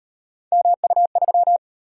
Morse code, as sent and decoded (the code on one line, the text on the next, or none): -- ..- ...--
MU3